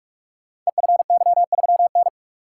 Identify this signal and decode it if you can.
Morse code, heard as a keyed tone, about 44 wpm, 700 Hz. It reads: EFY3N
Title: Code Groups - 5 Letter:Numbers - 45wpm